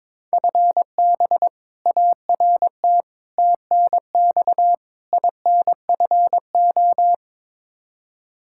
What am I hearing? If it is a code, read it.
Morse code, heard as a keyed tone, about 22 wpm, 700 Hz. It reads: FB ART TNX INFO
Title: K1USN SST QSO Practice 22wpm